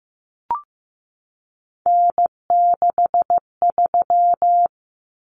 Morse code, heard as a keyed tone, about 15 wpm, 700 Hz. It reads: N63
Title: Common Cut Numbers - 3-Digit 15wpm